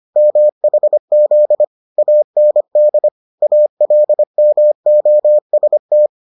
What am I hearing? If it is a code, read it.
Morse code, heard as a keyed tone, about 25 wpm, 600 Hz. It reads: MHZ AND ALMOST